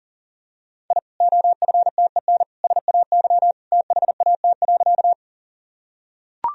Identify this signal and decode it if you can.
Morse code, heard as a keyed tone, about 37 wpm, 700 Hz. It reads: I OFTEN SAY THAT.
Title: Sentences from Top 1000 Words - Review Part 1 - 40wpm